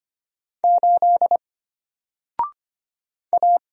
Morse code, heard as a keyed tone, about 25 wpm, 700 Hz. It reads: 8 A
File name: Single-Letter-Number-25wpm